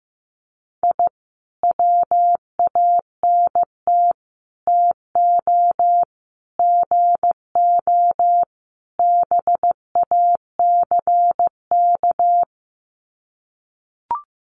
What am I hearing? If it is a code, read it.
Morse code, heard as a keyed tone, about 15 wpm, 700 Hz. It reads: I WANT TO GO BACK
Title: Sets of 5 Words Encore 15wpm